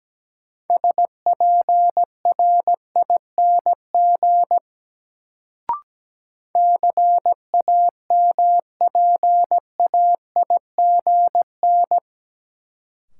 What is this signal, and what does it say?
Morse code, heard as a keyed tone, about 17 wpm, 700 Hz. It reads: SPRING CAMPAIGN
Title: Sentences from Top 1200 Words - New Word Drill - 17wpm